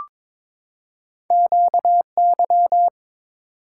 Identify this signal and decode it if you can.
Morse code, heard as a keyed tone, about 22 wpm, 690 Hz. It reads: QY
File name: practice / 22wpm / Two Letter-Number Combinations - 22wpm